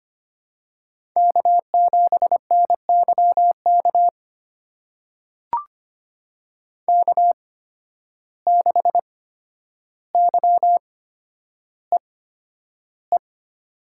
Morse code, heard as a keyed tone, about 25 wpm, spaced as for 11 wpm, 700 Hz. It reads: K7NYK K6YEE
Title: USA Call Signs - 5 Characters - 8x Character Spacing - 25wpm